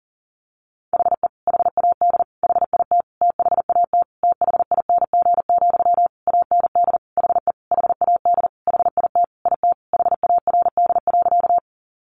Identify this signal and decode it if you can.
Morse code, heard as a keyed tone, about 40 wpm, 700 Hz. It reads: HE HAD HIT THAT THING, AND HE HAD HIT IT HARD.
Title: Sentences from Top 1200 Words 40wpm